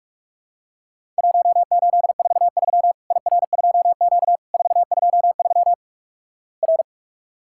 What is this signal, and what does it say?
Morse code, heard as a keyed tone, about 44 wpm, 700 Hz. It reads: 1843 IR2Q 423 R
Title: Worked All Europe DX Contest - QTC Exchange - 45wpm